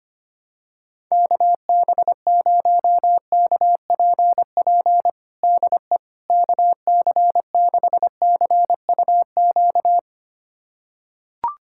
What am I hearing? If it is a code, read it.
Morse code, heard as a keyed tone, about 25 wpm, 700 Hz. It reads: KB0KPP DE KC6CUQ
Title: USA Callsign DE Callsign 25wpm